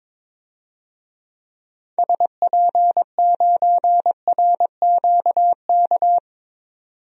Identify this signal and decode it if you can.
Morse code, heard as a keyed tone, about 21 wpm, 700 Hz. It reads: SP9RQK
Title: DXCC-100 Call Signs - 6 Characters - 22wpm